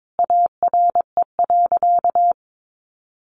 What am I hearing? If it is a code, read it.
Morse code, heard as a keyed tone, about 22 wpm, 700 Hz. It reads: ARE.
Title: Sentences from Top 700 Words 22wpm